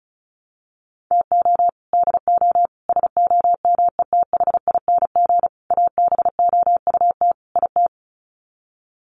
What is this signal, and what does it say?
Morse code, heard as a keyed tone, about 34 wpm, 700 Hz. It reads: TO DO SOMETHING ABOUT IT
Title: Sets of 5 Words 35wpm